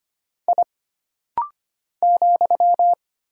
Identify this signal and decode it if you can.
Morse code, heard as a keyed tone, about 25 wpm, 700 Hz. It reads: I ,
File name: Single Character (Letter:Number:Punctuation:Prosign) - Rapid-Fire - 25wpm